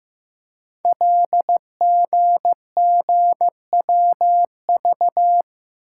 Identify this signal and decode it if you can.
Morse code, heard as a keyed tone, about 15 wpm, 700 Hz. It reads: LGGWV
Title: Code Groups - 5 Letters - 15wpm